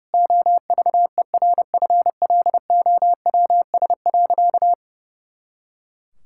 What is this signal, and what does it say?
Morse code, heard as a keyed tone, about 30 wpm, 700 Hz. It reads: OVERFLOWS.
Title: Pangram Challenge v2 - 30wpm